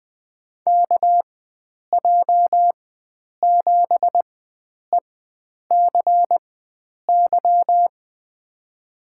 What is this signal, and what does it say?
Morse code, heard as a keyed tone, about 20 wpm, spaced as for 9 wpm, 700 Hz. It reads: KJ7ECY
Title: USA Call Signs - 6 Characters - 4x Character Spacing - 20wpm